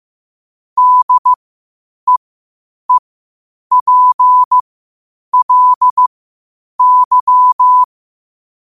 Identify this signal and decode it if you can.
Morse code, heard as a keyed tone, about 15 wpm, spaced as for 8 wpm, 1000 Hz. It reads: DEEPLY
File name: qotdfiles / 8wpm